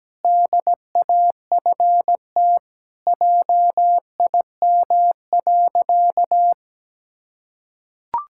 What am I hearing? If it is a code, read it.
Morse code, heard as a keyed tone, about 17 wpm, 700 Hz. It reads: DAFT JIM.
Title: Pangram Challenge v2 - 17wpm